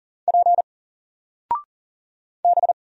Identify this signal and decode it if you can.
Morse code, heard as a keyed tone, about 40 wpm, 700 Hz. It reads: P B
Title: Single Character (Letter:Number:Punctuation:Prosign) Alt 40wpm